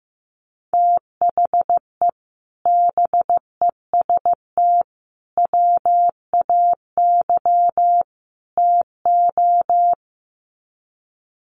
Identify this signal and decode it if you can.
Morse code, heard as a keyed tone, about 15 wpm, 700 Hz. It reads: THE BEST WAY TO
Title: Sets of 4 Words 15wpm